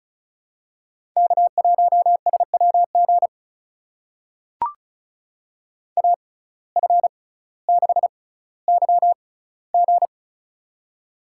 Morse code, heard as a keyed tone, about 36 wpm, spaced as for 18 wpm, 700 Hz. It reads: K1SWG AF6YG
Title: USA Call Signs - 5 Characters - 6x Character Spacing - 35wpm